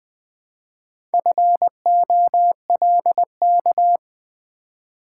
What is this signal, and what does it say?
Morse code, heard as a keyed tone, about 20 wpm, 700 Hz. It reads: FOLK
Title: Build A Word - 2 Letters At A Time - 4-Letter Words - 20wpm